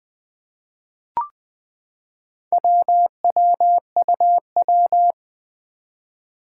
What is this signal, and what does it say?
Morse code, heard as a keyed tone, about 20 wpm, 700 Hz. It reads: WWUW
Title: Four Challenging Letter-Number Combinations 20wpm